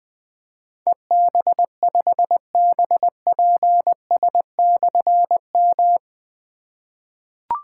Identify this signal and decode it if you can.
Morse code, heard as a keyed tone, about 20 wpm, 700 Hz. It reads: EB5BPS/M
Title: Difficult Call Signs 20wpm